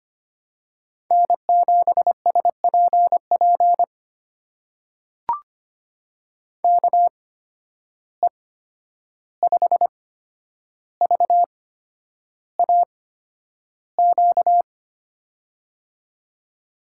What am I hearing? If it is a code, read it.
Morse code, heard as a keyed tone, about 25 wpm, spaced as for 10 wpm, 700 Hz. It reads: N7SPP KE5VAQ
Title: USA Call Signs - 8x Character Spacing - 25wpm